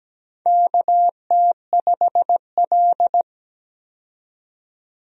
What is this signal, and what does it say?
Morse code, heard as a keyed tone, about 17 wpm, 700 Hz. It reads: KT5L
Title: USA Call Signs - 4 Characters - 17wpm